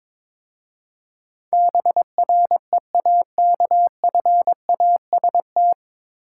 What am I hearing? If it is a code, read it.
Morse code, heard as a keyed tone, about 22 wpm, 700 Hz. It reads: BREAKFAST
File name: 9 Letter Words - First 3 Letters Given - 22wpm